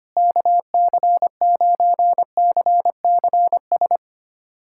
Morse code, heard as a keyed tone, about 25 wpm, 700 Hz. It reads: KC9CCS
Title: USA Callsign DE Callsign 25wpm